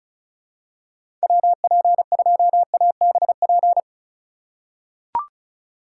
Morse code, heard as a keyed tone, about 36 wpm, 700 Hz. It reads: WP2ABP
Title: USA Call Signs - 6 Characters - 4x Character Spacing - 35wpm